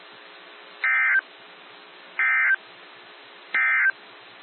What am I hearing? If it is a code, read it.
Emergency Alert System (SAME) tones, heard as harsh buzzing bursts. It is heard from 0.7 to 4.0 s.
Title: NOAA Weather Radio - WXJ73